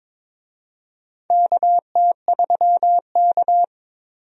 Morse code, heard as a keyed tone, about 22 wpm, 700 Hz. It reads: KT3K
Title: USA Call Signs - 4 Characters - 22wpm